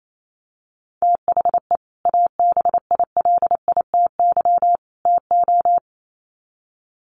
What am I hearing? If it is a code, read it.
Morse code, heard as a keyed tone, about 28 wpm, 700 Hz. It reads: THE ABILITY TO
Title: Sets of 3 Words 28wpm